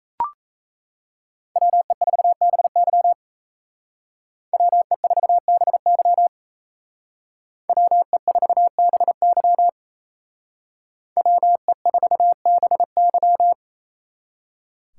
Morse code, heard as a keyed tone, about 34 wpm, 700 Hz. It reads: WE4BY WE4BY WE4BY WE4BY
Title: Speed Racing - USA 2x2 Call Signs 28wpm